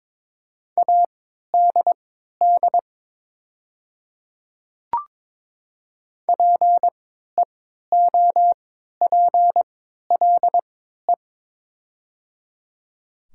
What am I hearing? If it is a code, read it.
Morse code, heard as a keyed tone, about 22 wpm, spaced as for 13 wpm, 700 Hz. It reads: ADD PEOPLE
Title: 3x Character Spacing - Top 200 Words - 22wpm